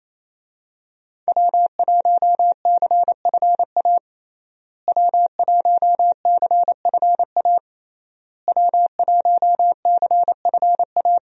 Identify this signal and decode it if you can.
Morse code, heard as a keyed tone, about 27 wpm, 700 Hz. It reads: W1CFA W1CFA W1CFA
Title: USA Call Signs - 5 Characters - Repeated 3x - 28wpm